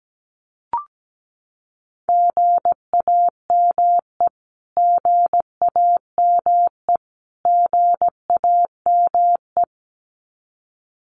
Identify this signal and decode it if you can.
Morse code, heard as a keyed tone, about 17 wpm, 700 Hz. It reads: GAME GAME GAME
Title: Top 500 Words Repeated 17wpm